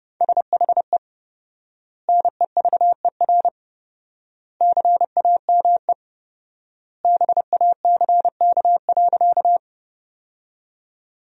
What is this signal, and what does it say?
Morse code, heard as a keyed tone, about 30 wpm, 700 Hz. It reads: SHE NEVER CAME BACK.
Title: 4x Word-Spacing - Sentences from Top 300 Words - 30wpm